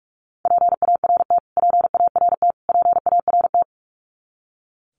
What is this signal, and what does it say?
Morse code, heard as a keyed tone, about 46 wpm, 700 Hz. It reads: PART PART PART
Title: Top 200 Words Repeated 45wpm